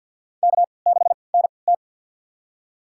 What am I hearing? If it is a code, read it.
Morse code, heard as a keyed tone, about 50 wpm, spaced as for 28 wpm, 700 Hz. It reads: K6NT
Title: USA Call Signs - 3x Character Spacing - 50wpm